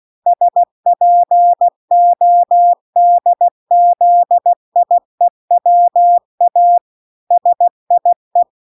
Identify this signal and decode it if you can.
Morse code, heard as a keyed tone, about 16 wpm, 700 Hz. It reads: SPODZIEWA SIE